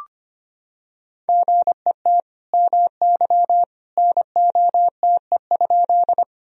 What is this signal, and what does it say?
Morse code, heard as a keyed tone, about 25 wpm, 700 Hz. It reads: GET MY NOTE?
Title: Sentences from Top 1000 Words - Review Part 2 - 25wpm